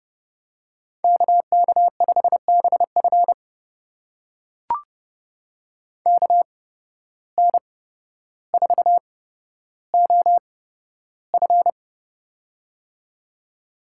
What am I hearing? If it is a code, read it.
Morse code, heard as a keyed tone, about 32 wpm, spaced as for 13 wpm, 700 Hz. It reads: KK5BF KN4OF
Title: USA Call Signs - 5 Characters - 8x Character Spacing - 30wpm